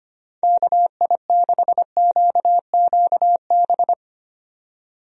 Morse code, heard as a keyed tone, about 25 wpm, 700 Hz. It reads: KI6QQB
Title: USA Call Signs - 6 Characters - 25wpm